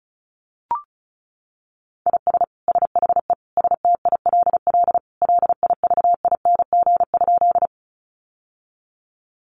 Morse code, heard as a keyed tone, about 34 wpm, 700 Hz. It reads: IS SHE STILL LIVING?
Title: Sentences from Top 1100 Words 35wpm